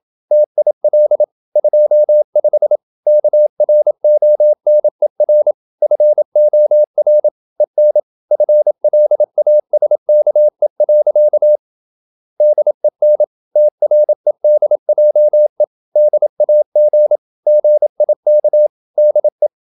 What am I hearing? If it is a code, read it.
Morse code, heard as a keyed tone, about 28 wpm, 600 Hz. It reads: TIL 25 KRONER FOR EN FLASKE. DEN TREDJE DAG GIK DE
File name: Ferie QRQ 27wpm